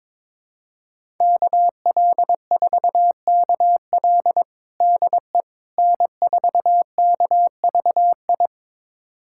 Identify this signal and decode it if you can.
Morse code, heard as a keyed tone, about 21 wpm, 700 Hz. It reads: KL4KL DE N4KVI
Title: USA Callsign DE Callsign - 5 Characters - 22wpm